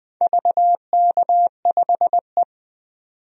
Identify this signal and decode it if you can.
Morse code, heard as a keyed tone, about 20 wpm, 700 Hz. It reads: VK5E
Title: DXCC-100 Call Signs - 4 Characters - 20wpm